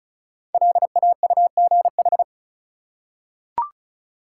Morse code, heard as a keyed tone, about 36 wpm, 700 Hz. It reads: LAUGH
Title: Five Letter Words - 35wpm